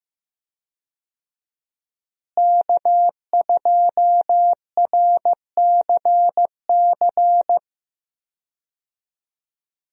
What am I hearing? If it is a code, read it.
Morse code, heard as a keyed tone, about 15 wpm, 700 Hz. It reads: K2RCC
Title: USA Call Signs - 5 Characters - Repeated 2x - 15wpm